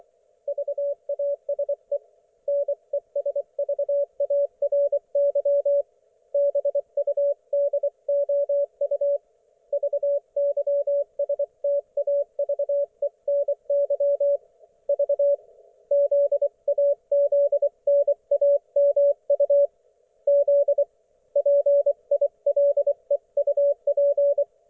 Morse code, heard as a keyed tone, about 24 wpm, 570 Hz. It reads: VASE NESVARY BUDOU VYSTAVENY V ZAZNAMU Z PILEUP
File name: Archiv / CUC224 / ok1fhi cul224